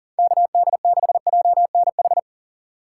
Morse code, heard as a keyed tone, about 40 wpm, 700 Hz. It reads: KD6JNH
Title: USA Call Signs - 6 Characters - 6x Character Spacing - 40wpm